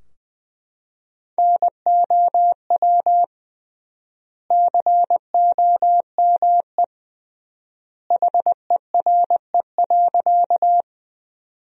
Morse code, heard as a keyed tone, about 20 wpm, 700 Hz. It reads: NOW COME HERE.
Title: 3x Word-Spacing - Sentences from Top 200 Words - 20wpm